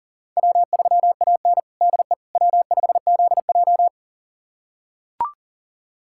Morse code, heard as a keyed tone, about 40 wpm, 700 Hz. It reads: W3AN DE W5ZJ
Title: USA Callsign DE Callsign - 4 Characters - 40wpm